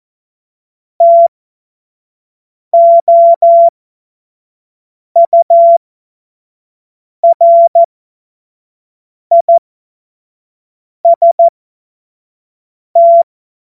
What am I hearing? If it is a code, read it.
Morse code, heard as a keyed tone, about 14 wpm, spaced as for 5 wpm, 680 Hz. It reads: TOURIST
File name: Witz-09-September-2025-25-F70